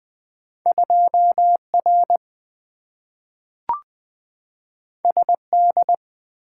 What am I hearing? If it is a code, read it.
Morse code, heard as a keyed tone, about 20 wpm, 700 Hz. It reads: 2R SD
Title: Two Letter-Number Combinations - 20wpm